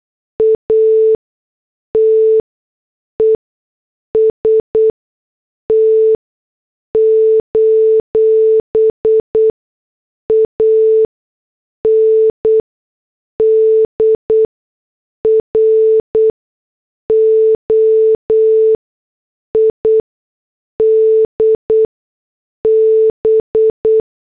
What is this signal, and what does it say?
Morse code, heard as a keyed tone, about 8 wpm, spaced as for 6 wpm, 440 Hz. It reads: ATEST: ANDROID B
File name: morse / headlines